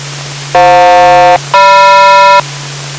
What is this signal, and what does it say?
Aircraft SELCAL, heard as two pairs of tones. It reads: FJ-GM